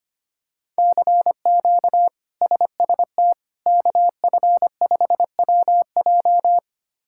Morse code, heard as a keyed tone, about 25 wpm, 700 Hz. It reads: CQ SST KF5WJ
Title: K1USN SST QSO Practice - 25wpm